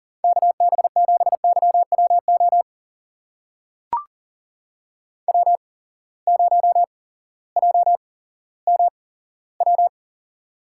Morse code, heard as a keyed tone, about 40 wpm, spaced as for 18 wpm, 700 Hz. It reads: KB7YWO W0JMW
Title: USA Call Signs - 8x Character Spacing - 40wpm